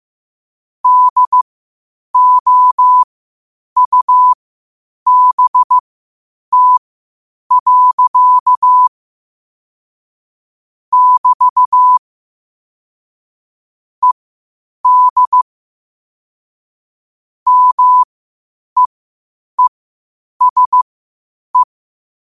Morse code, heard as a keyed tone, about 15 wpm, spaced as for 8 wpm, 1000 Hz. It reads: DOUBT. <BT> ED MEESE
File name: qotdfiles / 8wpm